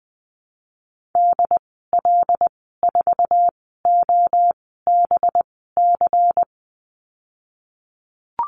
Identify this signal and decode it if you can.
Morse code, heard as a keyed tone, about 19 wpm, spaced as for 14 wpm, 700 Hz. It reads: DL4OBC
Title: 186 Lesson 30g - Letter b - Call Signs 20wpm